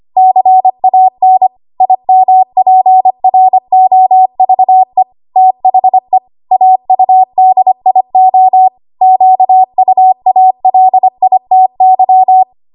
Morse code, heard as a keyed tone, about 25 wpm, 750 Hz. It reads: CAN IMPROVE THE AUDIO QUALITY
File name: Archive / 25 WPM / 25WPM